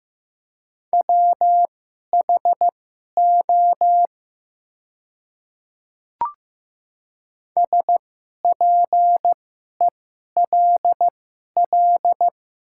Morse code, heard as a keyed tone, about 15 wpm, spaced as for 11 wpm, 690 Hz. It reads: WHO SPELL
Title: 2x Character Spacing - Top 200 Words - 15wpm